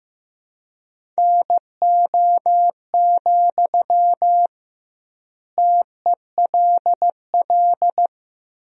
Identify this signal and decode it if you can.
Morse code, heard as a keyed tone, about 15 wpm, 700 Hz. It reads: NO, TELL
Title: 2x Word-Spacing - Sentences from Top 200 Words - 15wpm